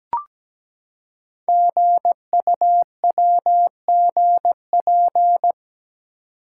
Morse code, heard as a keyed tone, about 17 wpm, 700 Hz. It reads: GUWGP
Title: Code Groups - 5 Letters - 17wpm